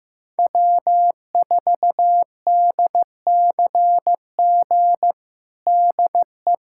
Morse code, heard as a keyed tone, about 15 wpm, 700 Hz. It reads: W4DCG DE
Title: USA Callsign DE Callsign - 5 Characters - 15wpm